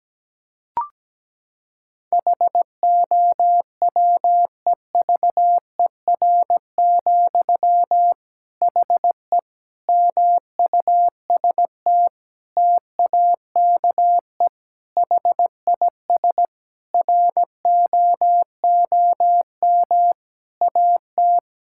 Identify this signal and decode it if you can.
Morse code, heard as a keyed tone, about 17 wpm, 700 Hz. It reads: HOWEVER, HE MUST TAKE HIS ROOM AT
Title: Sentences from Top 1600 Words 17wpm